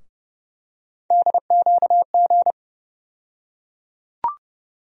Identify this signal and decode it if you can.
Morse code, heard as a keyed tone, about 30 wpm, 700 Hz. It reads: DQG
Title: USA Call Sign Postfixes - 3 Letters 30wpm